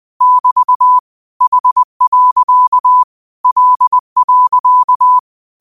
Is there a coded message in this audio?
Yes: Morse code from 0.2 to 5.2 s, heard as a keyed tone, about 20 wpm, 1000 Hz. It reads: <BT> H. L.